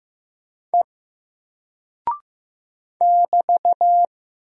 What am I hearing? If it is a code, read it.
Morse code, heard as a keyed tone, about 15 wpm, 700 Hz. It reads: E <BT>
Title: Single Character (Letter:Number:Punctuation:Prosign) 15wpm